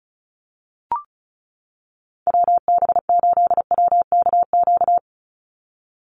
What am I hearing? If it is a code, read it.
Morse code, heard as a keyed tone, about 34 wpm, 700 Hz. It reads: WB8WKQ
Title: Call Signs - 35wpm